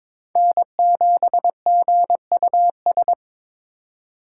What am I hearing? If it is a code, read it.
Morse code, heard as a keyed tone, about 22 wpm, 700 Hz. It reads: N7GUS
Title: USA Call Signs - 5 Characters - 4x Character Spacing - 22wpm